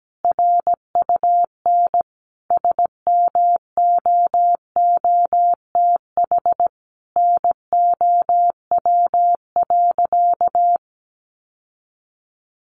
Morse code, heard as a keyed tone, about 17 wpm, 700 Hz. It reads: RUN SMOOTH NOW.
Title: Sentences from Top 1400 Words 17wpm